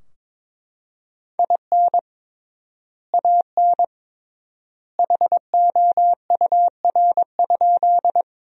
Morse code, heard as a keyed tone, about 22 wpm, 700 Hz. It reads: IN AN HOUR?
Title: 3x Word-Spacing - Sentences from Top 400 Words - 22wpm